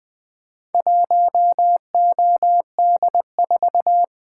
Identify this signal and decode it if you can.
Morse code, heard as a keyed tone, about 20 wpm, 700 Hz. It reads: 1OD4
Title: Four Letter-Number Combinations - 20wpm